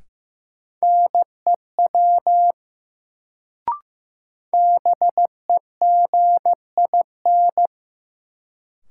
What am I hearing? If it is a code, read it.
Morse code, heard as a keyed tone, about 15 wpm, 700 Hz. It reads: NEW BEGIN
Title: Sentences from Top 300 Words - New Word Drill - 15wpm